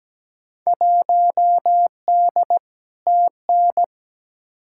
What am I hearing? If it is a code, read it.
Morse code, heard as a keyed tone, about 17 wpm, 700 Hz. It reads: 1D TN
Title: Field Day Exchanges - 17wpm